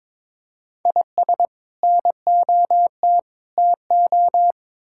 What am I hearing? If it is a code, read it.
Morse code, heard as a keyed tone, about 22 wpm, 700 Hz. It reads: IS NOT TO
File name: practice / 22wpm / Build A Phrase - 5-Word Phrases - 22wpm